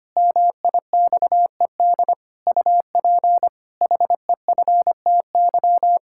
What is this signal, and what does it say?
Morse code, heard as a keyed tone, about 25 wpm, 700 Hz. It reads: MIXED UP HEFTY
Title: Pangram Challenge v2 - 25wpm